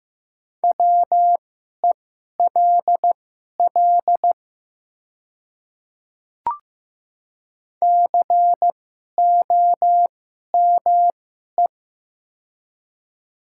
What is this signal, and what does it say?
Morse code, heard as a keyed tone, about 15 wpm, spaced as for 11 wpm, 700 Hz. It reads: WELL COME
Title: 2x Character Spacing - Top 300 Words - 15wpm